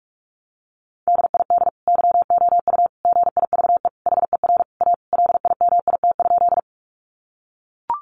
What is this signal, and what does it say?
Morse code, heard as a keyed tone, about 44 wpm, 700 Hz. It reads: DID YOU GIVE HER A LIMIT?
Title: Sentences from Top 1500 Words 45wpm